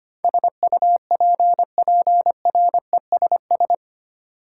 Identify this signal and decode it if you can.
Morse code, heard as a keyed tone, about 25 wpm, 700 Hz. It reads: SUPPRESS
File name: Eight Letter Words - 25wpm